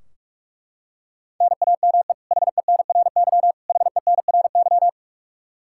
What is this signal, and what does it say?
Morse code, heard as a keyed tone, about 42 wpm, 700 Hz. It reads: NAME HENRY HENRY
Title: Practice QSO - Exchange 1 - Beginning - 45wpm